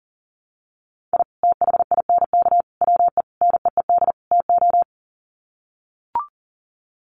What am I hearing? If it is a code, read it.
Morse code, heard as a keyed tone, about 40 wpm, 700 Hz. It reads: I THINK WE NEED TO
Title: Sets of 5 Words 40wpm